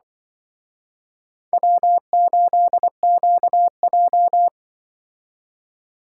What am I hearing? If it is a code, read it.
Morse code, heard as a keyed tone, about 23 wpm, 700 Hz. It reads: W8QJ
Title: Speed Racing - USA 1x2 Call Signs 20wpm